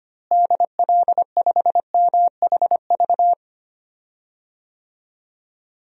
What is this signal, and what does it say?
Morse code, heard as a keyed tone, about 25 wpm, 700 Hz. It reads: DL5MHV
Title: DXCC-100 Call Signs - 6 Characters - Repeated 2x - 25wpm